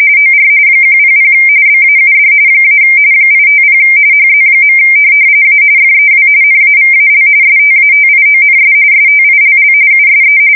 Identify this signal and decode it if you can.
RTTY, about 45.45 baud. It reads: OR 100 BAUD - 850 HZ. A FEW NAVAL STATIONS STILL USE
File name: fsk170 / teletype-RTTY